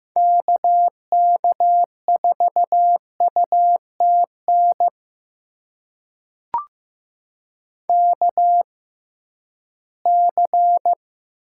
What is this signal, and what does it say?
Morse code, heard as a keyed tone, about 15 wpm, spaced as for 11 wpm, 700 Hz. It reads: KK4UTN KC
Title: USA Call Signs - 6 Characters - 6x Character Spacing - 15wpm